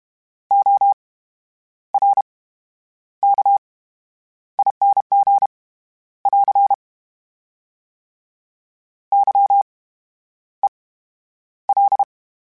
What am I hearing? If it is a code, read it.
Morse code, heard as a keyed tone, about 32 wpm, spaced as for 10 wpm, 800 Hz. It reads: ORKING <AR> YEL
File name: Special INGs / Special INGs-mp3s / VerySlowToFaster Ing @30